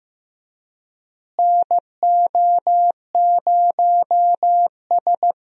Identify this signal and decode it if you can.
Morse code, heard as a keyed tone, about 16 wpm, 700 Hz. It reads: NO0S
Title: USA Call Signs - 4 Characters - Repeated 3x - 15wpm